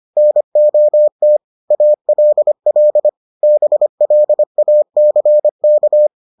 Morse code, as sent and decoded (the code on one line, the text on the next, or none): -. --- - / .- .-.. .-.. / -... .-.. .- -.-. -.-
NOT ALL BLACK